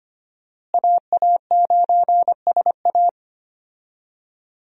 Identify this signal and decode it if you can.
Morse code, heard as a keyed tone, about 25 wpm, 700 Hz. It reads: AA9SA
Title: USA Call Signs - 5 Characters - Repeated 2x - 25wpm